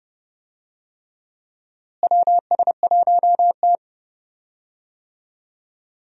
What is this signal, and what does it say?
Morse code, heard as a keyed tone, about 29 wpm, 700 Hz. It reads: WS1T
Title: USA Call Signs - 4 Characters - Repeated 3x - 30wpm